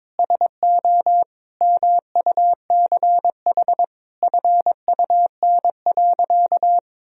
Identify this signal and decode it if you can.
Morse code, heard as a keyed tone, about 22 wpm, 700 Hz. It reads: SO MUCH FUN.
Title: Sentences from Top 1000 Words - Review Part 4 - 22wpm